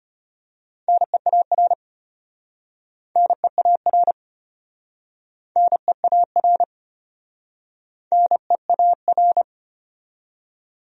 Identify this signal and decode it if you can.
Morse code, heard as a keyed tone, about 34 wpm, 700 Hz. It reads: NEAR NEAR NEAR NEAR